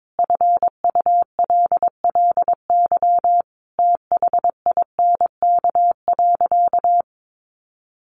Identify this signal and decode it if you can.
Morse code, heard as a keyed tone, about 22 wpm, 700 Hz. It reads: FULLY THINK.
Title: Sentences from Top 1300 Words 22wpm